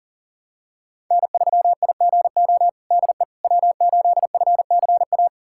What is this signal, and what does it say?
Morse code, heard as a keyed tone, about 40 wpm, 700 Hz. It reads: N3IGO DE W8FCA
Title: USA Callsign DE Callsign - 5 Characters - 40wpm